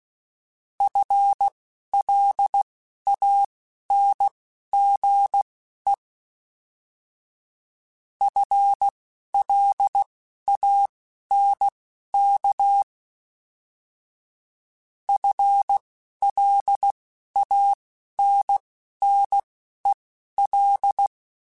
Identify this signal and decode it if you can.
Morse code, heard as a keyed tone, about 16 wpm, spaced as for 11 wpm, 800 Hz. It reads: FLANGE FLANK FLANNEL